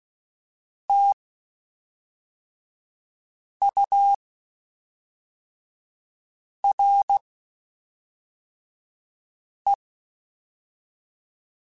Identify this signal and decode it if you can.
Morse code, heard as a keyed tone, about 16 wpm, 800 Hz. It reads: T U R E
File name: Spaced Characters / Spaced Characters-mp3s / Weather5 1 sp.@5